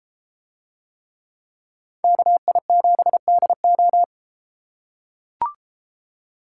Morse code, heard as a keyed tone, about 33 wpm, 700 Hz. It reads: KI7DO